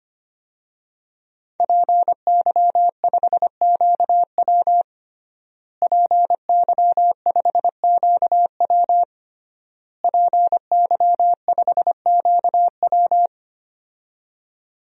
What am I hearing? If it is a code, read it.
Morse code, heard as a keyed tone, about 25 wpm, 700 Hz. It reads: PY5QW PY5QW PY5QW
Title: DXCC-100 Call Signs - 5 Characters - Repeated 3x - 25wpm